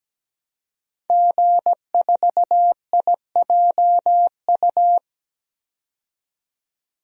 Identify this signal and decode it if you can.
Morse code, heard as a keyed tone, about 17 wpm, 700 Hz. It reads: G4IJU
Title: DXCC-100 Call Signs - 5 Characters - 17wpm